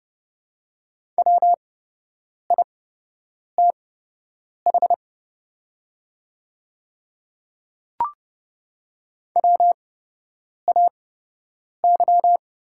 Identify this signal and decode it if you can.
Morse code, heard as a keyed tone, about 29 wpm, spaced as for 8 wpm, 690 Hz. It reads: WITH WAY
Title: 8x Character Spacing - Top 200 Words - 30wpm